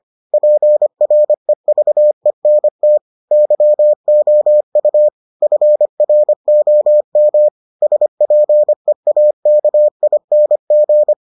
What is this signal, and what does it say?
Morse code, heard as a keyed tone, about 25 wpm, 600 Hz. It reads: PREVENT YOU FROM SPEAKING